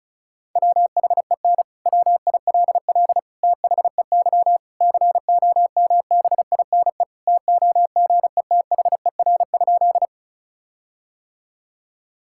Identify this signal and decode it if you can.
Morse code, heard as a keyed tone, about 36 wpm, 700 Hz. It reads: WHEN WILL THEY COMBINE TOGETHER?
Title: Sentences from Top 1600 Words 35wpm